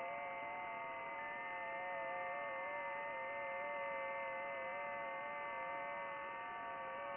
FT8.